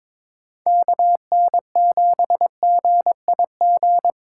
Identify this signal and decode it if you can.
Morse code, heard as a keyed tone, about 26 wpm, 700 Hz. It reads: KN7GIG
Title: Speed Racing - USA 2x3 Call Signs 22wpm